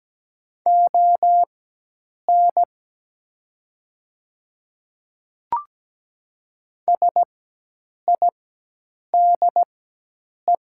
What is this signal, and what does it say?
Morse code, heard as a keyed tone, about 17 wpm, spaced as for 8 wpm, 700 Hz. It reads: ON SIDE
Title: 4x Character Spacing - Top 100 Words - 17wpm